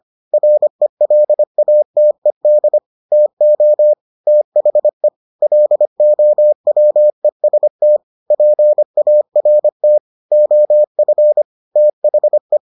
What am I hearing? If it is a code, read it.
Morse code, heard as a keyed tone, about 25 wpm, 600 Hz. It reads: RELATED TO THE LOWEST PART OF THE